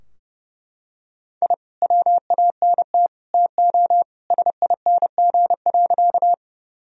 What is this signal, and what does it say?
Morse code, heard as a keyed tone, about 30 wpm, 700 Hz. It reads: I WANT TO SING.
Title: Sentences from Top 1000 Words - Review Part 2 - 30wpm